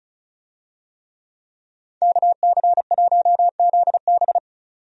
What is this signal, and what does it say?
Morse code, heard as a keyed tone, about 36 wpm, 700 Hz. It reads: KC1ZB
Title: USA Call Signs - 5 Characters - 35wpm